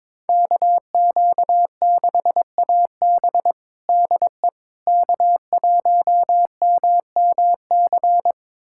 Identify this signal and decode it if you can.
Morse code, heard as a keyed tone, about 22 wpm, 700 Hz. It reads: KQ6AB DE K1MMC
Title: USA Callsign DE Callsign - 5 Characters - 22wpm